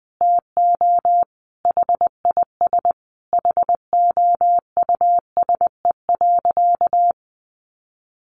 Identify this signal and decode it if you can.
Morse code, heard as a keyed tone, about 20 wpm, 700 Hz. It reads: TO HIS HOUSE.
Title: Additional Easy Phrases 20wpm